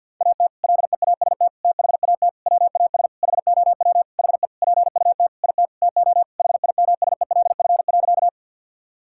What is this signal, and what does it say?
Morse code, heard as a keyed tone, about 51 wpm, 700 Hz. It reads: AT LEAST THAT WAS HOW HE PUT IT TO HIMSELF.
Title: Sentences from Top 1600 Words 50wpm